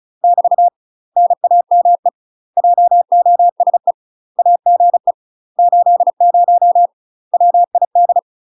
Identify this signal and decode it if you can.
Morse code, heard as a keyed tone, about 35 wpm, 700 Hz. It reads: <BT> NAME JOSE AGE 80 WID
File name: UK MASTER QSOs TRAINING/35WPM